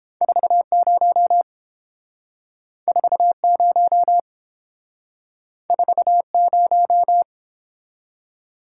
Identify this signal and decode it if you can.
Morse code, heard as a keyed tone, about 31 wpm, 700 Hz. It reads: 40 40 40